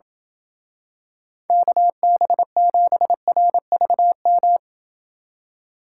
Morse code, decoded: KB7RVM

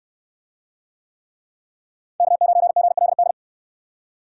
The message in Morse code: K9GLD

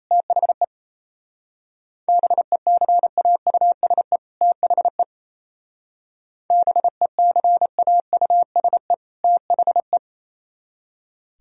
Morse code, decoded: THE BECAUSE THE BECAUSE THE

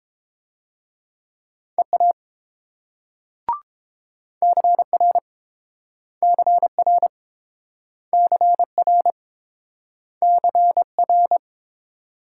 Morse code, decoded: EA CR CR CR CR

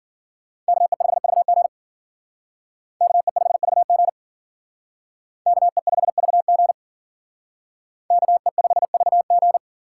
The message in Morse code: KE5VG KE5VG KE5VG KE5VG

